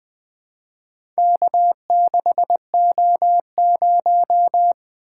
Morse code, heard as a keyed tone, about 20 wpm, 700 Hz. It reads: K6O0
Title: Four Letter-Number Combinations - 20wpm